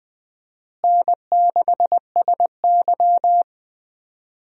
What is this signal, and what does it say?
Morse code, heard as a keyed tone, about 20 wpm, 700 Hz. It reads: N6SY
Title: USA Call Signs - 4 Characters - Repeated 3x - 20wpm